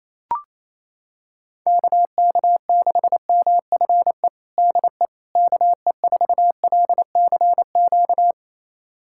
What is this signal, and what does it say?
Morse code, heard as a keyed tone, about 28 wpm, 700 Hz. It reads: KK6MFE DE KE4LCQ
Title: USA Callsign DE Callsign - 6 Characters - 28wpm